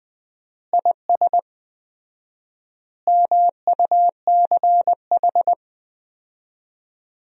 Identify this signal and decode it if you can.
Morse code, heard as a keyed tone, about 21 wpm, 700 Hz. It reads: IS MUCH
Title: 4x Word-Spacing - Sentences from Top 400 Words - 20wpm